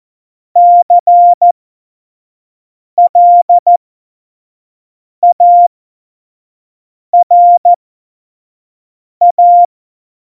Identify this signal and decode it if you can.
Morse code, heard as a keyed tone, about 14 wpm, spaced as for 5 wpm, 700 Hz. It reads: CLARA
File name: Morsekurs / Witze / Witz 06. Oktober 2020, 25 F70